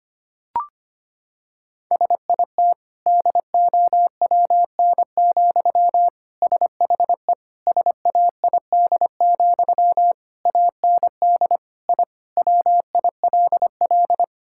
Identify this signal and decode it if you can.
Morse code, heard as a keyed tone, about 25 wpm, 700 Hz. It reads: SIT DOWN, SHE SAID, AND I WILL
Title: Sentences from Top 1000 Words - Review Part 6 - 25wpm